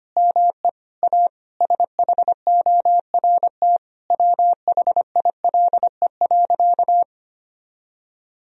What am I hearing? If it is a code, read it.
Morse code, heard as a keyed tone, about 25 wpm, 700 Hz. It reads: ME A SHORT WHILE.
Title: Sentences from Top 1000 Words - Review Part 2 - 25wpm